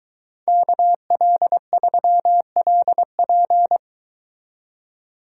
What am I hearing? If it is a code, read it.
Morse code, heard as a keyed tone, about 24 wpm, 700 Hz. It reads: KL3LP